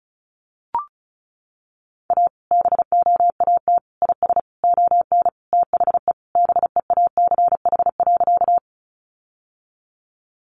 Morse code, heard as a keyed tone, about 34 wpm, 700 Hz. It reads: A BOAT IS ON THE BEACH.
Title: Sentences from Top 1500 Words 35wpm